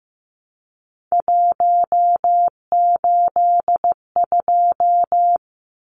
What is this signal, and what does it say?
Morse code, heard as a keyed tone, about 15 wpm, 700 Hz. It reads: 182
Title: Numbers - 3-Digit 15wpm